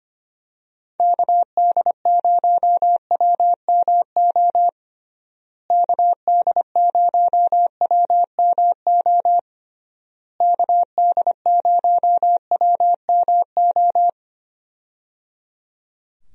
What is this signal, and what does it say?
Morse code, heard as a keyed tone, about 25 wpm, 700 Hz. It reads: KD0WMO KD0WMO KD0WMO
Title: USA Call Signs - 6 Characters - Repeated 3x - 25wpm